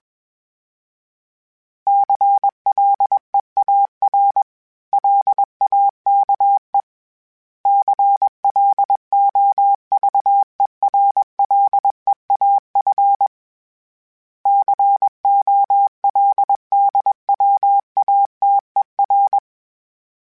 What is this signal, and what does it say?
Morse code, heard as a keyed tone, about 21 wpm, 800 Hz. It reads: CLEAR LAKE CLOVERLEAF COLDWATER
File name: fromSMs / FromSMs mp3s / Cities@20